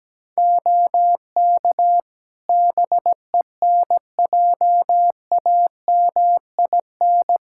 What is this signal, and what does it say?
Morse code, heard as a keyed tone, about 17 wpm, 700 Hz. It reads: OK BENJAMIN
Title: K1USN SST QSO Practice 17wpm